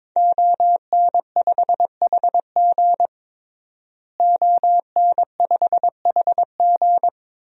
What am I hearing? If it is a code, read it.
Morse code, heard as a keyed tone, about 22 wpm, 700 Hz. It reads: ON5HG ON5HG
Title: DXCC-100 Call Signs - 5 Characters - Repeated 3x - 22wpm